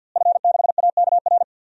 Morse code, heard as a keyed tone, about 52 wpm, 700 Hz. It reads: F6AKR
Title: DXCC-100 Call Signs - 5 Characters - Repeated 2x - 50wpm